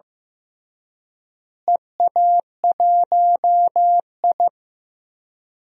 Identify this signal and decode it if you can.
Morse code, heard as a keyed tone, about 15 wpm, 700 Hz. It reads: EA1I
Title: DXCC-100 Call Signs - 4 Characters - Repeated 3x - 15wpm